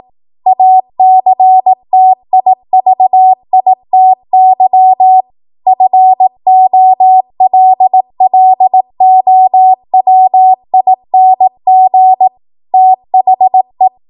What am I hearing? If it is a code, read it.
Morse code, heard as a keyed tone, about 18 wpm, 750 Hz. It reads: ACTIVITY FOLLOWING THE